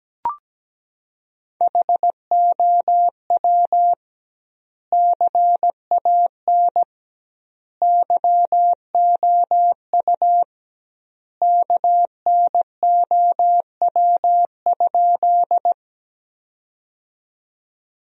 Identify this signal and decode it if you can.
Morse code, heard as a keyed tone, about 17 wpm, 700 Hz. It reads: HOW CAN YOU KNOW?
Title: 2x Word-Spacing - Sentences from Top 100 Words - 17wpm